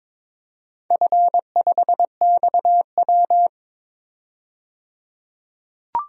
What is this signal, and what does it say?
Morse code, heard as a keyed tone, about 22 wpm, 700 Hz. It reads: F5XW
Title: DXCC-100 Call Signs - 4 Characters - 22wpm